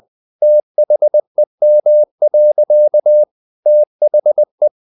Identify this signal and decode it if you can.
Morse code, heard as a keyed tone, about 20 wpm, 600 Hz. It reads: THEM. THE
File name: the old man and the sea part 2 20wpm